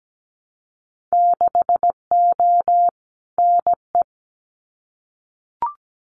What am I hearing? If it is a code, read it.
Morse code, heard as a keyed tone, about 17 wpm, 700 Hz. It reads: 6O NE
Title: Winter Field Day Exchanges Encore 17wpm